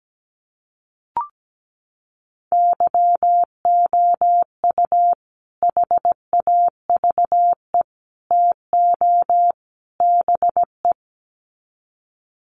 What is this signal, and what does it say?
Morse code, heard as a keyed tone, about 17 wpm, 700 Hz. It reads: YOU HAVE TO BE
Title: Sets of 4 Words 17wpm